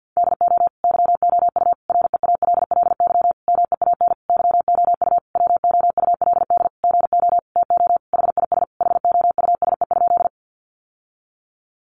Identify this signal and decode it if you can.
Morse code, heard as a keyed tone, about 50 wpm, 700 Hz. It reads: DO YOU REALLY MEAN YOU WOULD GO TO HIS HOUSE?
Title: Sentences from Top 600 Words 50wpm